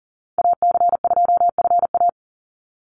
Morse code, heard as a keyed tone, about 40 wpm, 700 Hz. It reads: AC2FA
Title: Call Signs - 40wpm